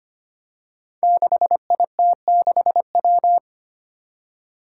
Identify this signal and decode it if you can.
Morse code, heard as a keyed tone, about 25 wpm, 700 Hz. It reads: 6IT6W
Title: Five Challenging Letter-Number Combinations 25wpm